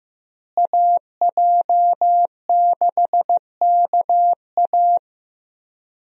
Morse code, heard as a keyed tone, about 15 wpm, 700 Hz. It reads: AJ6KA
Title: USA Call Signs - 5 Characters - Repeated 2x - 15wpm